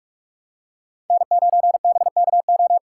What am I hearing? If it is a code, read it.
Morse code, heard as a keyed tone, about 40 wpm, 700 Hz. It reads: N9BKO